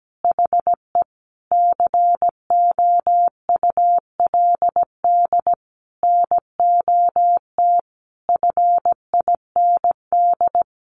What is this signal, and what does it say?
Morse code, heard as a keyed tone, about 17 wpm, 700 Hz. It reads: HE COULD NOT FIND
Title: Sentences from Top 1400 Words 17wpm